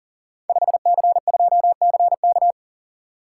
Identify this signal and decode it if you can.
Morse code, heard as a keyed tone, about 41 wpm, 690 Hz. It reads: 5C2CK
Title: Code Groups - 5 Letter:Numbers - 40wpm